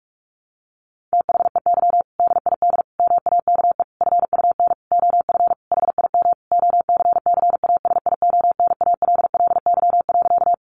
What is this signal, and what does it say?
Morse code, heard as a keyed tone, about 44 wpm, 700 Hz. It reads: THEY DID MAKE FUN OF HIM OCCASIONALLY.
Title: Sentences from Top 1300 Words 45wpm